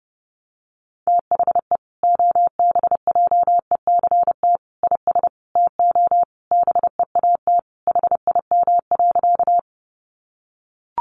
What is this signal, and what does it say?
Morse code, heard as a keyed tone, about 30 wpm, 700 Hz. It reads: THE OBJECT IS TO BEAT HIM.
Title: Sentences from Top 1300 Words 30wpm